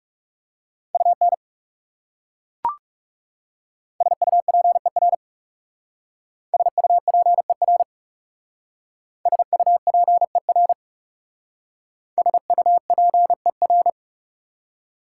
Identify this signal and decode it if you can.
Morse code, heard as a keyed tone, about 36 wpm, 690 Hz. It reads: UN SUPER SUPER SUPER SUPER
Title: Speed-Racing Prefixes and Suffixes 30wpm